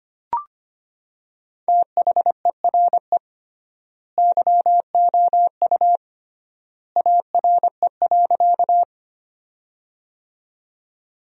Morse code, heard as a keyed tone, about 25 wpm, 700 Hz. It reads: THERE YOU ARE.
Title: 3x Word-Spacing - Sentences from Top 100 Words - 25wpm